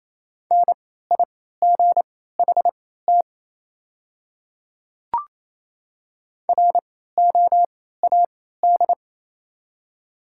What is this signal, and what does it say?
Morse code, heard as a keyed tone, about 28 wpm, spaced as for 16 wpm, 700 Hz. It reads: NIGHT ROAD
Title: 3x Character Spacing - Top 500 Words - 28wpm